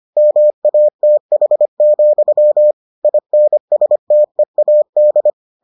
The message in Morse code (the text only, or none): MATH, INSTEAD